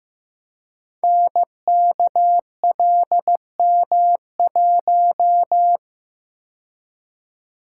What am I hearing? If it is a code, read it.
Morse code, heard as a keyed tone, about 15 wpm, 700 Hz. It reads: NKLM1
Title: Code Groups - 5 Letter:Numbers - 15wpm